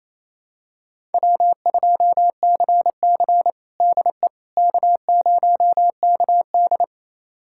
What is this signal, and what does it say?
Morse code, heard as a keyed tone, about 28 wpm, 700 Hz. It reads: W2CC DE K0KD
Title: USA Callsign DE Callsign - 4 Characters - 28wpm